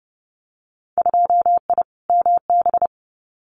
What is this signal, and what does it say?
Morse code, heard as a keyed tone, about 30 wpm, 700 Hz. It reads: 2I MB
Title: Winter Field Day Exchanges Encore 30wpm